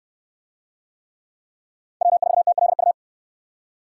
Morse code, heard as a keyed tone, about 67 wpm, 700 Hz. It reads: W4TLU